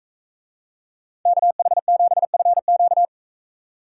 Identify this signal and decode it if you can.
Morse code, heard as a keyed tone, about 44 wpm, 700 Hz. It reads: KH7FQ